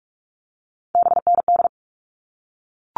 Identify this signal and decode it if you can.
Morse code, heard as a keyed tone, about 44 wpm, 700 Hz. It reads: BND
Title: All Cut Numbers - 3-Digit 45wpm